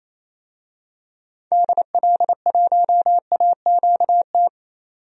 Morse code, heard as a keyed tone, about 28 wpm, 700 Hz. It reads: DL1AQT
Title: DXCC-100 Call Signs - 6 Characters - 28wpm